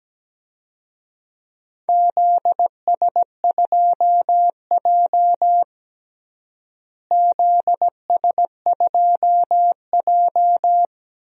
Morse code, heard as a keyed tone, about 17 wpm, 700 Hz. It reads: ZS2J ZS2J
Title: DXCC-100 Call Signs - 4 Characters - Repeated 3x - 17wpm